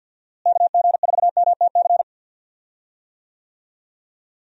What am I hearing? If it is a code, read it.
Morse code, heard as a keyed tone, about 50 wpm, 700 Hz. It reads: KG4MTC
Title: USA Call Signs - 6 Characters - Repeated 3x - 50wpm